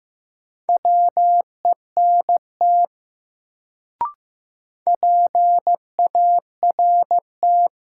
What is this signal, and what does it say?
Morse code, heard as a keyed tone, about 15 wpm, 700 Hz. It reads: WENT PART
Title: Sets of 2 Words - New Word Drill - 15wpm